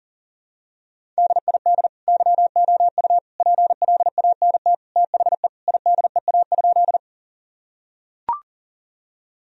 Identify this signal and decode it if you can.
Morse code, heard as a keyed tone, about 40 wpm, 700 Hz. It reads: DID YOU PLANT THE IDEA?
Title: Sentences from Top 1000 Words - Review Part 1 - 40wpm